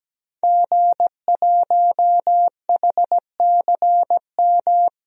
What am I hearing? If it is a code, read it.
Morse code, heard as a keyed tone, about 17 wpm, 700 Hz. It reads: G1HCM
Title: DXCC-100 Call Signs - 5 Characters - 17wpm